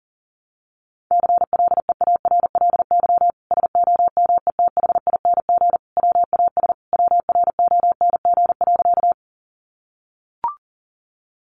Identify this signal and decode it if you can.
Morse code, heard as a keyed tone, about 37 wpm, 700 Hz. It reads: CLEARLY SOMETHING WAS WRONG.
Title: Sentences from Top 1300 Words 40wpm